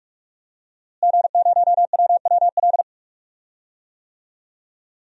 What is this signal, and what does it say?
Morse code, heard as a keyed tone, about 46 wpm, 700 Hz. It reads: G0WWL
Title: DXCC-100 Call Signs - 5 Characters - Repeated 3x - 45wpm